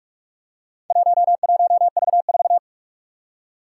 Morse code, heard as a keyed tone, about 46 wpm, 700 Hz. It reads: 11V4
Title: Four Challenging Letter-Number Combinations 45wpm